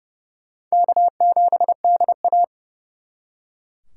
Morse code, heard as a keyed tone, about 39 wpm, 700 Hz. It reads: K7DA